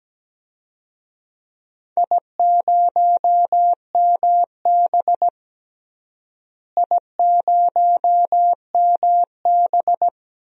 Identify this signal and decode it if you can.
Morse code, heard as a keyed tone, about 17 wpm, 700 Hz. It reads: I0MB I0MB
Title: DXCC-100 Call Signs - 4 Characters - Repeated 3x - 17wpm